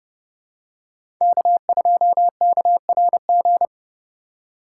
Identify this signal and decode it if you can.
Morse code, heard as a keyed tone, about 32 wpm, 700 Hz. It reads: K2KRG
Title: USA Call Signs - 5 Characters - 2x Character Spacing - 30wpm